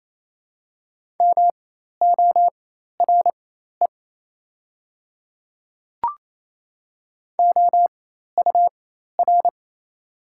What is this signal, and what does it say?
Morse code, heard as a keyed tone, about 27 wpm, spaced as for 13 wpm, 700 Hz. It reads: MORE OUR
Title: 4x Character Spacing - Top 200 Words - 28wpm